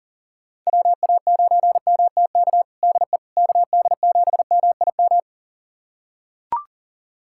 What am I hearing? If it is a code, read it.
Morse code, heard as a keyed tone, about 37 wpm, 700 Hz. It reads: WA9MTK DE KD7MIM
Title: USA Callsign DE Callsign - 6 Characters - 40wpm